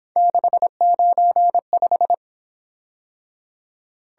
Morse code, heard as a keyed tone, about 26 wpm, 700 Hz. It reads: 695
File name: Speed-Racing - Three Letter-Number Combinations - 22wpm